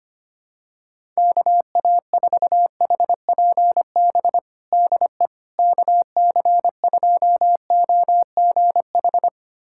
Morse code, heard as a keyed tone, about 25 wpm, 700 Hz. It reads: KA4HPB DE KC2OGH
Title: USA Callsign DE Callsign 25wpm